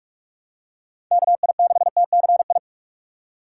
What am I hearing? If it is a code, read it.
Morse code, heard as a keyed tone, about 42 wpm, 700 Hz. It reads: KI6TCI
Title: USA Call Signs - 2x Character Spacing - 45wpm